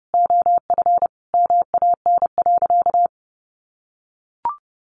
Morse code, heard as a keyed tone, about 30 wpm, 700 Hz. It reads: OF MAN.
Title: Sentences from Top 1500 Words 30wpm